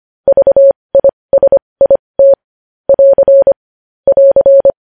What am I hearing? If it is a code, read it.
Morse code, heard as a keyed tone, about 25 wpm, spaced as for 20 wpm, 550 Hz. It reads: VISIT <AR> <AR>
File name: podcast / News headlines-1x-25x20